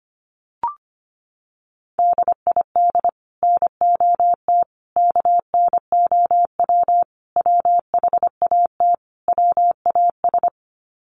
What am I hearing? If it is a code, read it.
Morse code, heard as a keyed tone, about 25 wpm, 700 Hz. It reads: DID NOT KNOW WHAT WAS
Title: Sets of 5 Words Encore2 25wpm